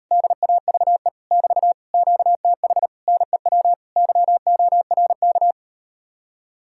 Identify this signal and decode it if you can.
Morse code, heard as a keyed tone, about 39 wpm, 700 Hz. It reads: DAVE <BT> QTH NEW YORK